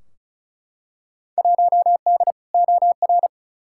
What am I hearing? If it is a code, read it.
Morse code, heard as a keyed tone, about 36 wpm, 700 Hz. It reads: 1D OR
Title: Field Day Exchanges - 35wpm